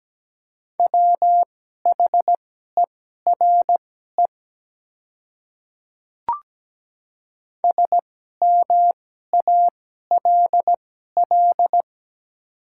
Morse code, heard as a keyed tone, about 17 wpm, spaced as for 12 wpm, 700 Hz. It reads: WHERE SMALL
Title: 2x Character Spacing - Top 200 Words - 17wpm